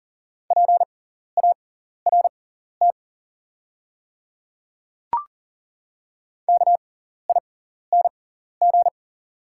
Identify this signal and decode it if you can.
Morse code, heard as a keyed tone, about 39 wpm, spaced as for 14 wpm, 690 Hz. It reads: PART KING
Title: 6x Character Spacing - Top 500 Words - 40wpm